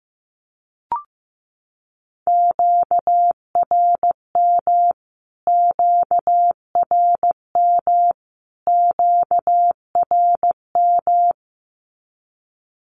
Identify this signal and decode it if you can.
Morse code, heard as a keyed tone, about 15 wpm, 700 Hz. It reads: QRM QRM QRM
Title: Q-Codes Repeated 3x 15wpm